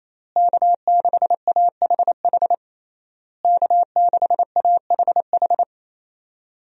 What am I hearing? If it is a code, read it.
Morse code, heard as a keyed tone, about 28 wpm, 700 Hz. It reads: K6AHH K6AHH
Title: USA Call Signs - 5 Characters - Repeated 3x - 28wpm